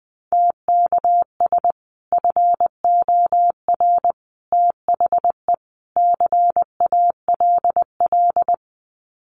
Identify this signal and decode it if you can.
Morse code, heard as a keyed tone, about 19 wpm, 700 Hz. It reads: TKS FOR THE CALL
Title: Common QSO Phrases 20wpm